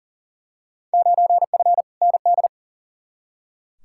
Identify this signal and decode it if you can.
Morse code, heard as a keyed tone, about 40 wpm, 700 Hz. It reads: 9F ND
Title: Field Day Exchanges Encore - 40wpm